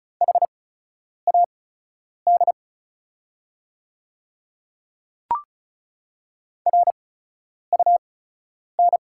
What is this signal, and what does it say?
Morse code, heard as a keyed tone, about 36 wpm, spaced as for 10 wpm, 700 Hz. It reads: HAD RUN
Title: 8x Character Spacing - Top 500 Words - 35wpm